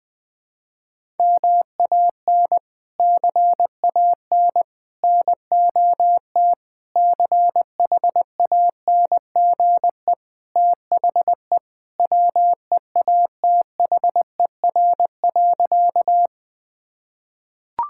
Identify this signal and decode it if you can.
Morse code, heard as a keyed tone, about 20 wpm, 700 Hz. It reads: MAN CAN NOT CHANGE THE WEATHER.
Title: Sentences from Top 1000 Words - Review Part 5 - 20wpm